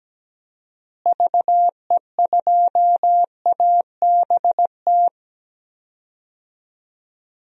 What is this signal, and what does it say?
Morse code, heard as a keyed tone, about 17 wpm, 700 Hz. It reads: VE2ABT
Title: DXCC-100 Call Signs - 6 Characters - Repeated 3x - 17wpm